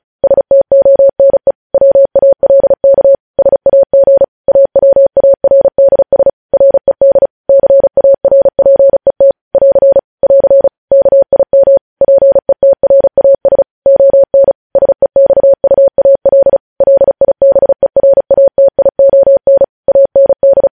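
Morse code, heard as a keyed tone, about 36 wpm, 550 Hz. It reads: STONE WALK SAG AWARDS RED CARPET <AR> <AR> KIM PETRAS ON SEXUAL LIBERATION AND
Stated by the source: Morse code transmission of news headlines, preceded by vvv.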